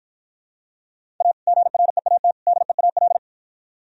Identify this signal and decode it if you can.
Morse code, heard as a keyed tone, about 52 wpm, 700 Hz. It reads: A GREAT DEAL